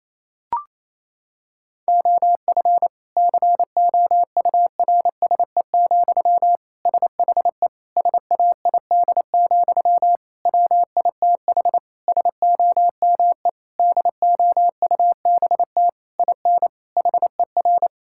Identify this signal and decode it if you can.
Morse code, heard as a keyed tone, about 28 wpm, 700 Hz. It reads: OF COURSE, SHE SAID, WITH SOME DOUBT IN HER
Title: Sentences from Top 1600 Words 28wpm